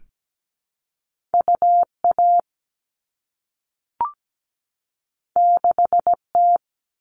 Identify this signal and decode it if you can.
Morse code, heard as a keyed tone, about 17 wpm, 700 Hz. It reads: UA 6T
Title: All Cut Numbers - 2-Digit 17wpm